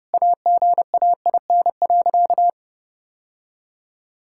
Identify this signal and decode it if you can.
Morse code, heard as a keyed tone, about 30 wpm, 700 Hz. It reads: AGAIN.
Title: Sentences from Top 1600 Words 30wpm